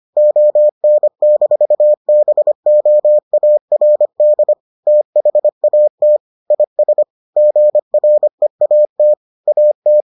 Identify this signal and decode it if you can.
Morse code, heard as a keyed tone, about 25 wpm, 600 Hz. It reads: ON-BOARD THAT IS GREAT AT